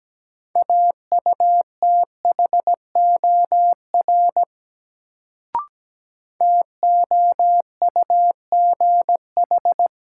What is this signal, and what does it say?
Morse code, heard as a keyed tone, about 17 wpm, 700 Hz. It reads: AUTHOR TOUGH
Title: Sentences from Top 1200 Words - New Word Drill - 17wpm